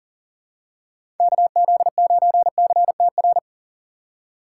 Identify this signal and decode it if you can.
Morse code, heard as a keyed tone, about 40 wpm, 700 Hz. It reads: KZ9CTR
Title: USA Call Signs - 6 Characters - 6x Character Spacing - 40wpm